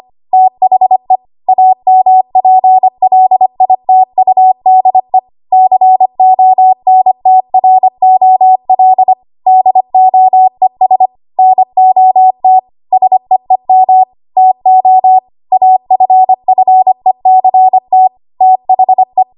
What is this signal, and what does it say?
Morse code, heard as a keyed tone, about 25 wpm, 750 Hz. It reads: THE AMPLITUDE CONTROL DOES NOT SEEM TO AFFECT THE
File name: Archive / 25 WPM / 25WPM